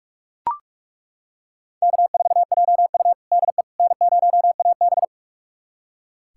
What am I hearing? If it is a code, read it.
Morse code, heard as a keyed tone, about 42 wpm, 700 Hz. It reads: K4JU DE N0AB
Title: USA Callsign DE Callsign - 4 Characters - 45wpm